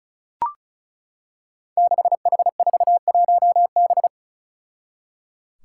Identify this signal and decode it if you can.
Morse code, heard as a keyed tone, about 35 wpm, 700 Hz. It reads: 6H41B
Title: Five Challenging Letter-Number Combinations 35wpm